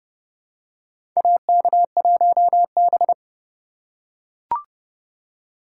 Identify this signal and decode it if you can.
Morse code, heard as a keyed tone, about 30 wpm, 700 Hz. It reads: AK1B
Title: USA Call Signs - 4 Characters - 3x Character Spacing - 30wpm